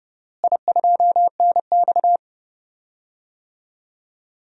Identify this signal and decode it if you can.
Morse code, heard as a keyed tone, about 30 wpm, 700 Hz. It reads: I2NX
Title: DXCC-100 Call Signs - 4 Characters - Repeated 2x - 30wpm